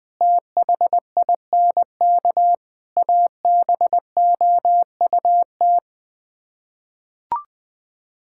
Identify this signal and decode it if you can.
Morse code, heard as a keyed tone, about 19 wpm, 700 Hz. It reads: THINK ABOUT
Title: Sets of 8 Words - 20wpm